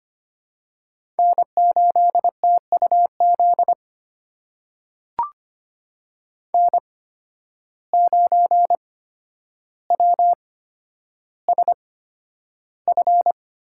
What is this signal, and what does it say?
Morse code, heard as a keyed tone, about 25 wpm, spaced as for 11 wpm, 700 Hz. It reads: N8TUZ N9WSF
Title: USA Call Signs - 5 Characters - 8x Character Spacing - 25wpm